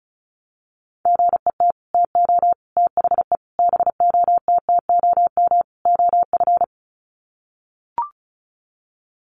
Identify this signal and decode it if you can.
Morse code, heard as a keyed tone, about 34 wpm, 700 Hz. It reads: GET TO THE BOTTOM OF
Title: Sets of 5 Words Encore 35wpm